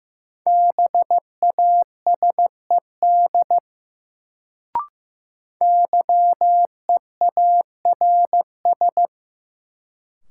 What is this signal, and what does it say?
Morse code, heard as a keyed tone, about 15 wpm, 700 Hz. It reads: BASED YEARS
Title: Sets of 2 Words - New Word Drill - 15wpm